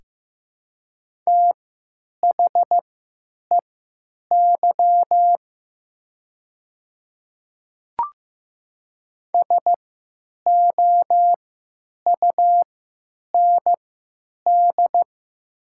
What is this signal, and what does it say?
Morse code, heard as a keyed tone, about 15 wpm, spaced as for 9 wpm, 700 Hz. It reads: THEY SOUND
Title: 3x Character Spacing - Top 100 Words - 15wpm